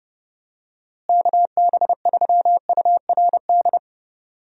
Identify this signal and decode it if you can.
Morse code, heard as a keyed tone, about 30 wpm, 700 Hz. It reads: KB3URD
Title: USA Call Signs - 6 Characters - Repeated 3x - 30wpm